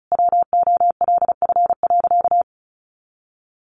Morse code, heard as a keyed tone, about 34 wpm, 700 Hz. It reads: WOLF.
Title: Additional Easy Phrases 35wpm